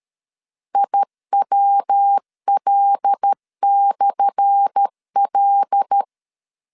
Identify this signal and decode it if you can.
Morse code, heard as a keyed tone, about 13 wpm, 800 Hz. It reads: IWL/L
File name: LEZIONI / 10 SIGNS / 10A SLASH 12wpm 2314